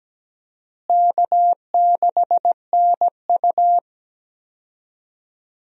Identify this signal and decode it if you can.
Morse code, heard as a keyed tone, about 17 wpm, 700 Hz. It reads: K6NU
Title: USA Call Signs - 4 Characters - Repeated 2x - 17wpm